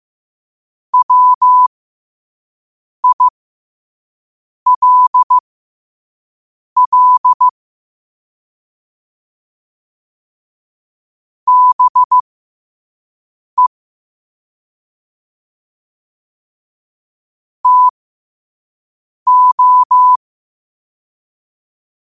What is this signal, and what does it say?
Morse code, heard as a keyed tone, about 15 wpm, spaced as for 5 wpm, 1000 Hz. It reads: WILL BE TO
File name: qotdfiles / 5wpm